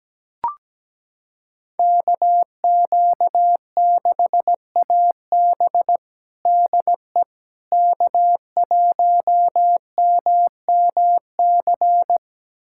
Morse code, heard as a keyed tone, about 17 wpm, 700 Hz. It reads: KQ6AB DE K1MMC
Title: USA Callsign DE Callsign - 5 Characters - 17wpm